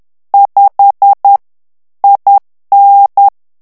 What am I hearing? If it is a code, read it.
Morse code, heard as a keyed tone, about 10 wpm, 800 Hz. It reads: 5 IN
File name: ANGLAIS texte clair 3 vitesse 10 mots minute